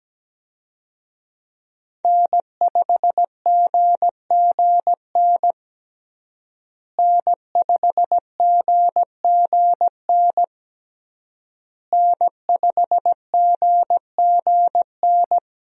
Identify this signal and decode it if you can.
Morse code, heard as a keyed tone, about 17 wpm, 700 Hz. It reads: N5GGN N5GGN N5GGN
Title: USA Call Signs - 5 Characters - Repeated 3x - 17wpm